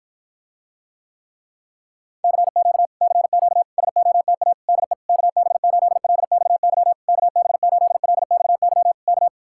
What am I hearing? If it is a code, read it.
Morse code, heard as a keyed tone, about 53 wpm, 700 Hz. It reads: CQ CQ SOTA DE KB8LXY KB8LXY K